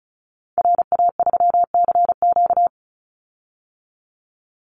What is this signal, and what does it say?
Morse code, heard as a keyed tone, about 34 wpm, 700 Hz. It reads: RA3CQ
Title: Call Signs - 35wpm